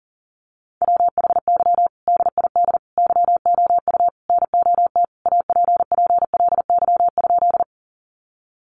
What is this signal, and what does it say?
Morse code, heard as a keyed tone, about 40 wpm, 700 Hz. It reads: WHY DID YOU NOT APPLY?
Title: Sentences from Top 1200 Words 40wpm